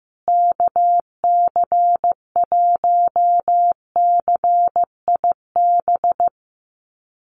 Morse code, heard as a keyed tone, about 15 wpm, 700 Hz. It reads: KC1CIB
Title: USA Call Signs Repeated 3x 15wpm